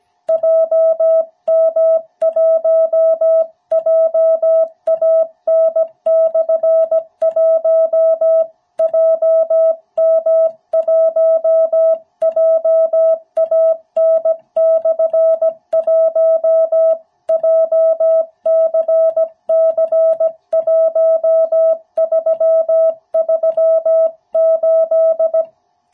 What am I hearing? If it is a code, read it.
Morse code, heard as a keyed tone, about 17 wpm, 660 Hz. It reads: JM1JAN/1 JM1JAN/1 JCC1338
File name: jh8cbh / 2021cw